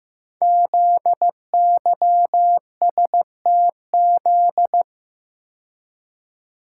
Morse code, heard as a keyed tone, about 15 wpm, 700 Hz. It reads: ZYSTZ
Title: Code Groups - 5 Letters - 15wpm